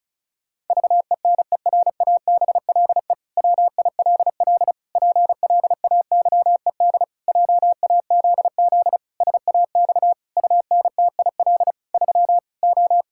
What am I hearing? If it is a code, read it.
Morse code, heard as a keyed tone, about 36 wpm, 700 Hz. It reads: VENERABLE WILL PLAYED JAZZ SAX UNTIL 3 O
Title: Pangram Challenge v2 - 35wpm